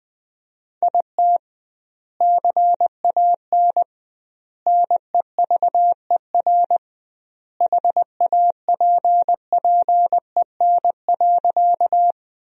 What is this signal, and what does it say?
Morse code, heard as a keyed tone, about 20 wpm, 700 Hz. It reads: IT CAN NEVER HAPPEN.
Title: 2x Word-Spacing - Sentences from Top 400 Words - 20wpm